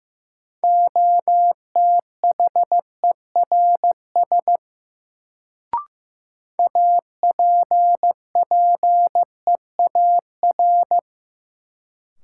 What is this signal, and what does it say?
Morse code, heard as a keyed tone, about 15 wpm, 700 Hz. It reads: OTHERS APPEAR
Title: Sentences from Top 1000 Words - Review Part 2 - New Word Drill - 15wpm